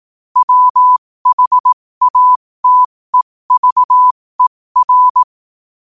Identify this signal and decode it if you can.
Morse code, heard as a keyed tone, about 18 wpm, spaced as for 15 wpm, 1000 Hz. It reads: WHATEVER
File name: qotdfiles / 15wpm